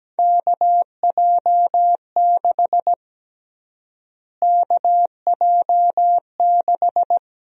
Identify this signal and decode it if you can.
Morse code, heard as a keyed tone, about 17 wpm, 700 Hz. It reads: KJ6 KJ6
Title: USA Call Sign Prefixes - Repeated 2x - 17wpm